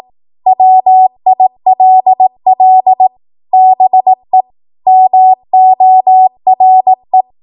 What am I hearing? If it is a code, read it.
Morse code, heard as a keyed tone, about 18 wpm, 750 Hz. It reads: WILL BE MORE
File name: Morse / 18WPM